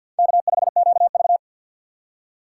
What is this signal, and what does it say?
Morse code, heard as a keyed tone, about 53 wpm, 700 Hz. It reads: K5QV
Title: USA Call Signs - 4x Character Spacing - 50wpm